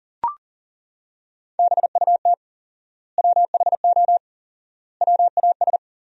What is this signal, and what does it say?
Morse code, heard as a keyed tone, about 40 wpm, 700 Hz. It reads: BUT WHO WAS
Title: 4x Word-Spacing - Sentences from Top 200 Words - 40wpm